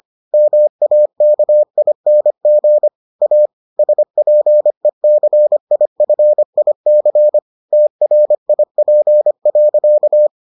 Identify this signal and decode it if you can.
Morse code, heard as a keyed tone, about 25 wpm, 600 Hz. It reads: MAKING A SPECIFIC TRIP.